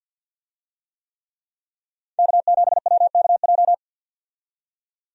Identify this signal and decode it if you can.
Morse code, heard as a keyed tone, about 52 wpm, 700 Hz. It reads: K7WKJ
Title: USA Call Signs - 5 Characters - 50wpm